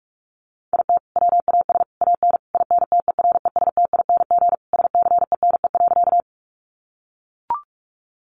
Morse code, heard as a keyed tone, about 44 wpm, 700 Hz. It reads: IT WAS AN INTERESTING SCENE.
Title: Sentences from Top 1100 Words 45wpm